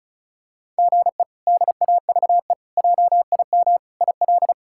Morse code, heard as a keyed tone, about 36 wpm, 700 Hz. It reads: GE DAVE JIM IL
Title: K1USN SST QSO Practice - 35wpm